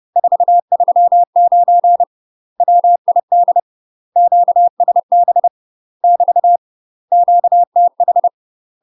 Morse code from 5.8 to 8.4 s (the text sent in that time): <BT> QTH